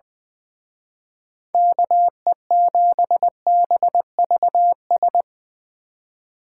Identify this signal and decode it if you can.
Morse code, heard as a keyed tone, about 22 wpm, 700 Hz. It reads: KE7BVS
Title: Speed Racing - USA Call Signs 20wpm